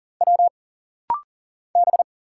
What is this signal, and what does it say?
Morse code, heard as a keyed tone, about 40 wpm, 700 Hz. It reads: W B
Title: Single Letter-Number Alt - Rapid-Fire 40wpm